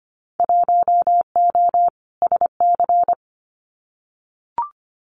Morse code, heard as a keyed tone, about 25 wpm, 700 Hz. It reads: 1O SC
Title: Winter Field Day Exchanges 25wpm